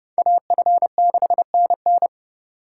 Morse code, heard as a keyed tone, about 30 wpm, 700 Hz. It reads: AF6NN